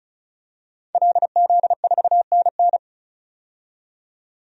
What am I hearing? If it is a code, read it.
Morse code, heard as a keyed tone, about 34 wpm, 700 Hz. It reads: LZ4NN
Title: DXCC-100 Call Signs - 5 Characters - 35wpm